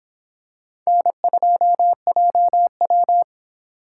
Morse code, heard as a keyed tone, about 25 wpm, 700 Hz. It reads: N2JW